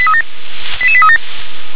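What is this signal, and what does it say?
Five-tone selective call (ZVEI), heard as quick runs of five tones, tones of about 70 ms. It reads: 88837
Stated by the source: ZVEI five-tone call received on SDR.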